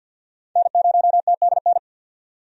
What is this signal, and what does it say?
Morse code, heard as a keyed tone, about 49 wpm, 700 Hz. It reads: N0TDN
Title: USA Call Signs - 5 Characters - 50wpm